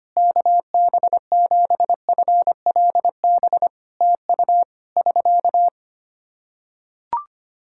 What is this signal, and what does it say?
Morse code, heard as a keyed tone, about 25 wpm, 700 Hz. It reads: KB7FLB TU <SK>
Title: Practice QSO - Exchange 4 - Ending - 25wpm